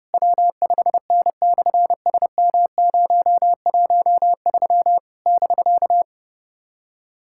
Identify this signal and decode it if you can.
Morse code, heard as a keyed tone, about 30 wpm, 700 Hz. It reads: W5N/SM013 <BK>
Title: SOTA QSO Practice 30wpm